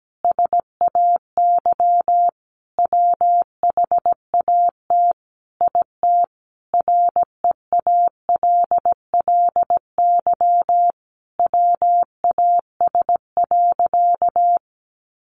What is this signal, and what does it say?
Morse code, heard as a keyed tone, about 17 wpm, 700 Hz. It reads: SAY WHAT IT REALLY WAS.
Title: Sentences from Top 600 Words 17wpm